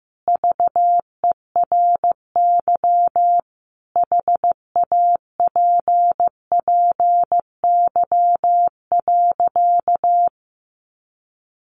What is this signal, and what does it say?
Morse code, heard as a keyed tone, about 15 wpm, 700 Hz. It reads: VERY HAPPY.
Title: Sentences from Top 1000 Words 15wpm